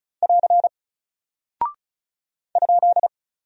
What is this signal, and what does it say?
Morse code, heard as a keyed tone, about 36 wpm, 700 Hz. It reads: <AR> ?
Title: Single Character (Letter:Number:Punctuation:Prosign) Alt 35wpm